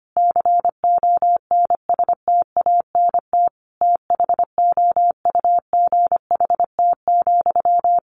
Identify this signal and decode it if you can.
Morse code, heard as a keyed tone, about 25 wpm, 700 Hz. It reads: CONSTANT THOUGHT,
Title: Sentences from Top 1500 Words 25wpm